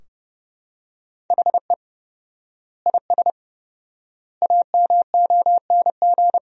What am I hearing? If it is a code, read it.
Morse code, heard as a keyed tone, about 32 wpm, 700 Hz. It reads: HE IS AMONG
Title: 4x Word-Spacing - Sentences from Top 500 Words - 30wpm